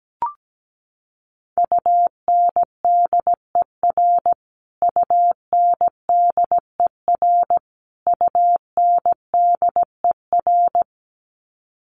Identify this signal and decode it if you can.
Morse code, heard as a keyed tone, about 17 wpm, 700 Hz. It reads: UNDER UNDER UNDER
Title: Top 500 Words Repeated 17wpm